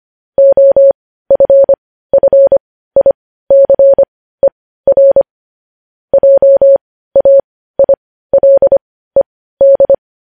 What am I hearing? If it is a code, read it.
Morse code, heard as a keyed tone, about 25 wpm, spaced as for 15 wpm, 550 Hz. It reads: OFFICER JAILED